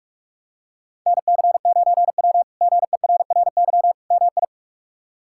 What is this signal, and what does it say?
Morse code, heard as a keyed tone, about 46 wpm, 700 Hz. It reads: NC9W GERRY MI
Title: K1USN SST QSO Practice - 45wpm